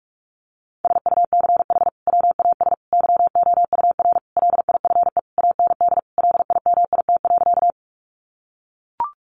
Morse code, heard as a keyed tone, about 44 wpm, 700 Hz. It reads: SUCH WAS YOUR LIFE AND LIMIT.
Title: Sentences from Top 1500 Words 45wpm